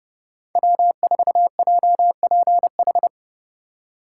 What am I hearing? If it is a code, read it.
Morse code, heard as a keyed tone, about 30 wpm, 700 Hz. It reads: W4JPH
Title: K1USN SST QSO Practice - 30wpm